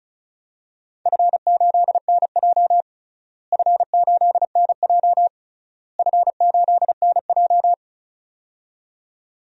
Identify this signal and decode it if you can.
Morse code, heard as a keyed tone, about 36 wpm, 700 Hz. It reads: F8NJ F8NJ F8NJ
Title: DXCC-100 Call Signs - 4 Characters - Repeated 3x - 35wpm